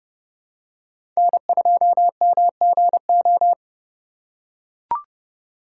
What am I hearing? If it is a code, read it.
Morse code, heard as a keyed tone, about 29 wpm, 700 Hz. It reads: N2MGO
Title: USA Call Signs - 4x Character Spacing - 30wpm